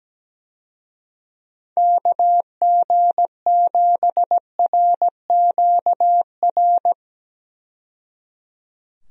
Morse code, heard as a keyed tone, about 17 wpm, 700 Hz. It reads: KG7RQR
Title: USA Call Signs - 6 Characters - 17wpm